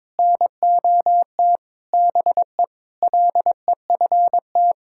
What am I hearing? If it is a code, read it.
Morse code, heard as a keyed tone, about 22 wpm, 700 Hz. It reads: NOT BE LEFT
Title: Sentences from Top 1000 Words - Review Part 5 - 22wpm